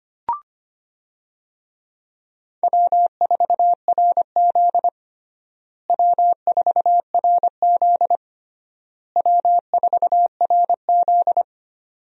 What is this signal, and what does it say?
Morse code, heard as a keyed tone, about 25 wpm, 700 Hz. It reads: W4RZ W4RZ W4RZ
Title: USA Call Signs - 4 Characters - Repeated 3x - 25wpm